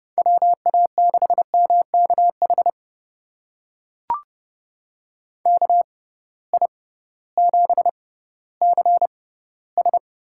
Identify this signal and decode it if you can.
Morse code, heard as a keyed tone, about 30 wpm, spaced as for 16 wpm, 700 Hz. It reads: WA6MKH KI7CS
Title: USA Call Signs - 6 Characters - 6x Character Spacing - 30wpm